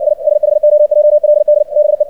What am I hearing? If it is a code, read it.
Morse code, heard as a keyed tone, about 57 wpm, 600 Hz. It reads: I SHOJLD BE
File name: archived / CRSnet